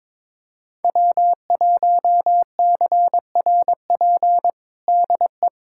Morse code, heard as a keyed tone, about 22 wpm, 700 Hz. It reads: W1CRP DE
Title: USA Callsign DE Callsign - 5 Characters - 22wpm